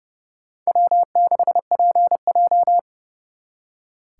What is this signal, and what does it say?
Morse code, heard as a keyed tone, about 36 wpm, 700 Hz. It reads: W6PJ
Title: Speed Racing - USA 1x2 Call Signs 30wpm